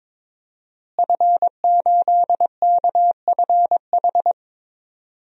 Morse code, heard as a keyed tone, about 22 wpm, 700 Hz. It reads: F8KFH
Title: DXCC-100 Call Signs - 5 Characters - Repeated 2x - 22wpm